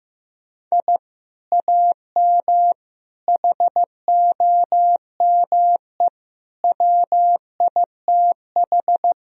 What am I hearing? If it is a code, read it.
Morse code, heard as a keyed tone, about 15 wpm, 700 Hz. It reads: I AM HOME WITH
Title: Sentences from Top 1000 Words - Review Part 3 - 15wpm